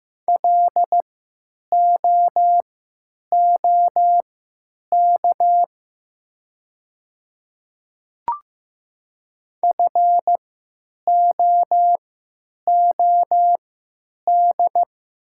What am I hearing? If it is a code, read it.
Morse code, heard as a keyed tone, about 15 wpm, spaced as for 9 wpm, 700 Hz. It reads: LOOK FOOD
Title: 3x Character Spacing - Top 300 Words - 15wpm